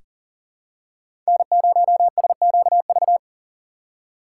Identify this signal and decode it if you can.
Morse code, heard as a keyed tone, about 40 wpm, 700 Hz. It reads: N0SQV
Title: USA Call Signs - 5 Characters - 3x Character Spacing - 40wpm